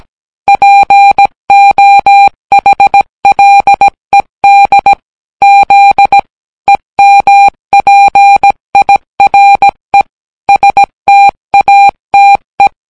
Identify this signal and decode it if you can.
Morse code, heard as a keyed tone, about 17 wpm, 800 Hz. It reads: POHLED Z EMPIRE STATE